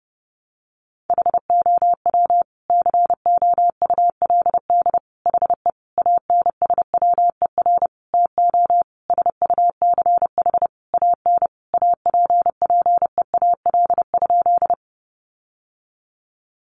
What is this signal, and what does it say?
Morse code, heard as a keyed tone, about 30 wpm, 700 Hz. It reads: HOW COULD HE ANSWER TO SUCH AN APPEAL?
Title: Sentences from Top 1300 Words 30wpm